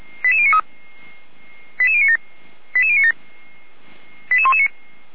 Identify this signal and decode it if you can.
Five-tone selective call (ZVEI), heard as quick runs of five tones, tones of about 70 ms. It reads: 88093 88097 88097 88119